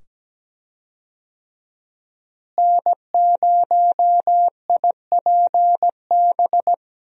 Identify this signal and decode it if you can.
Morse code, heard as a keyed tone, about 17 wpm, 700 Hz. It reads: N0IPB
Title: USA Call Signs - 5 Characters - Repeated 2x - 17wpm